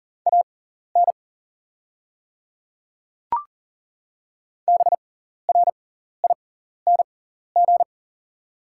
Morse code, heard as a keyed tone, about 40 wpm, spaced as for 14 wpm, 700 Hz. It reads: AN BRING
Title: 6x Character Spacing - Top 500 Words - 40wpm